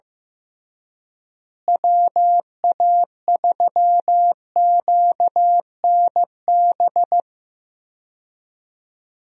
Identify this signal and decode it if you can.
Morse code, heard as a keyed tone, about 15 wpm, 700 Hz. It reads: WA3QNB
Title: USA Call Signs - 6 Characters - Repeated 2x - 15wpm